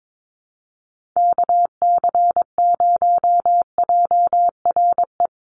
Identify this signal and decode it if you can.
Morse code, heard as a keyed tone, about 22 wpm, 700 Hz. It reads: KC0JRE
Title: USA Call Signs 22wpm